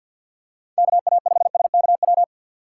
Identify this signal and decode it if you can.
Morse code, heard as a keyed tone, about 49 wpm, 700 Hz. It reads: KA5SKW
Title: USA Call Signs - 6 Characters - 4x Character Spacing - 50wpm